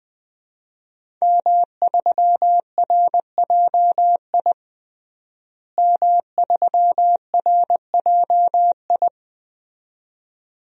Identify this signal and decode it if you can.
Morse code, heard as a keyed tone, about 20 wpm, 700 Hz. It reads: M3RJI M3RJI
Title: DXCC-100 Call Signs - 5 Characters - Repeated 2x - 20wpm